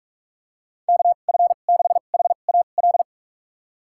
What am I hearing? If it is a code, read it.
Morse code, heard as a keyed tone, about 41 wpm, spaced as for 31 wpm, 700 Hz. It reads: KF6HAL
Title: USA Call Signs - 6 Characters - 2x Character Spacing - 45wpm